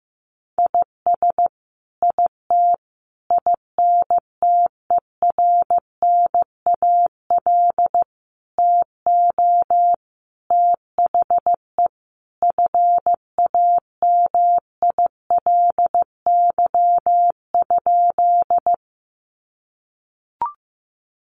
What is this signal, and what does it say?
Morse code, heard as a keyed tone, about 14 wpm, 700 Hz. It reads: IS IT INTERNAL TO THE FAMILY?
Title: Sentences from Top 1400 Words 15wpm